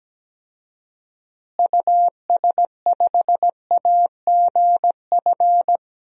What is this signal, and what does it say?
Morse code, heard as a keyed tone, about 17 wpm, 700 Hz. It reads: US5AGF
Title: DXCC-100 Call Signs - 6 Characters - Repeated 2x - 17wpm